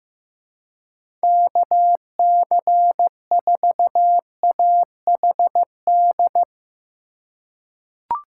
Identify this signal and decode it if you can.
Morse code, heard as a keyed tone, about 15 wpm, 700 Hz. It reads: KC4AHD
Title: USA Call Signs - 6 Characters - 3x Character Spacing - 15wpm